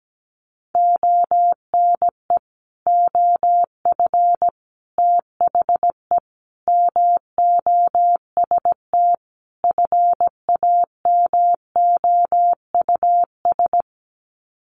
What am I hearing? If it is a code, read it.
Morse code, heard as a keyed tone, about 17 wpm, 700 Hz. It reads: ONE OF THE MOST FAMOUS
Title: Sets of 5 Words Encore2 17wpm